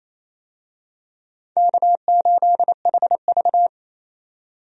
Morse code, heard as a keyed tone, about 28 wpm, 700 Hz. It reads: K8HV
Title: USA Call Signs - 4 Characters - 28wpm